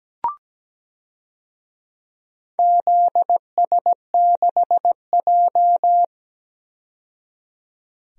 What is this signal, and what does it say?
Morse code, heard as a keyed tone, about 17 wpm, 700 Hz. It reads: ZS6J
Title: DXCC-100 Call Signs - 4 Characters - 17wpm